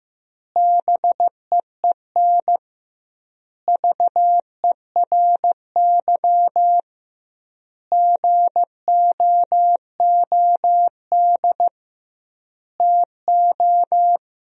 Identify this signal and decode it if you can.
Morse code, heard as a keyed tone, about 16 wpm, 700 Hz. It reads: BEEN VERY GOOD TO
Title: 2x Word-Spacing - Sentences from Top 200 Words - 15wpm